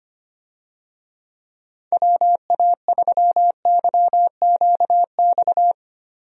Morse code, heard as a keyed tone, about 25 wpm, 700 Hz. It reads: WA3YQX
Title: USA Call Signs - 6 Characters - Repeated 2x - 25wpm